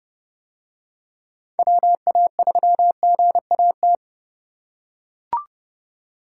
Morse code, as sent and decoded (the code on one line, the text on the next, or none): .-- .- ...-- --. .- -
WA3GAT